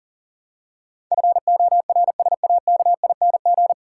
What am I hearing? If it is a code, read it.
Morse code, heard as a keyed tone, about 40 wpm, 700 Hz. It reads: FORSAKING
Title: Pangram Challenge v2 - 40wpm